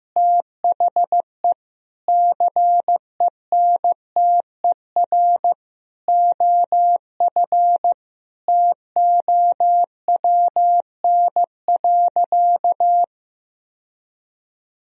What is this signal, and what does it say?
Morse code, heard as a keyed tone, about 15 wpm, 700 Hz. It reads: THE CENTER OF TOWN.
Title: Sentences from Top 1000 Words - Review Part 2 - 15wpm